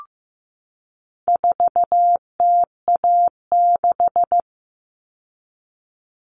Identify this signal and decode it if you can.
Morse code, heard as a keyed tone, about 15 wpm, 700 Hz. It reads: 4TA6
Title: All Cut Numbers - 4-Digit 15wpm